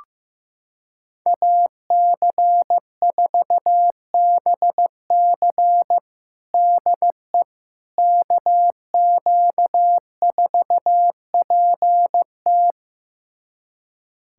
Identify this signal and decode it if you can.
Morse code, heard as a keyed tone, about 15 wpm, 700 Hz. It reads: AC4BC DE KQ4PT
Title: USA Callsign DE Callsign - 5 Characters - 15wpm